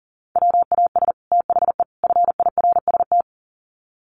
Morse code, heard as a keyed tone, about 40 wpm, 700 Hz. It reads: WAS THE FIRST
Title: Sets of 3 Words 40wpm